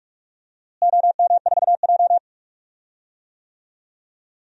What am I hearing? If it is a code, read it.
Morse code, heard as a keyed tone, about 46 wpm, 700 Hz. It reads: OM4J
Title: DXCC-100 Call Signs - 4 Characters - Repeated 3x - 45wpm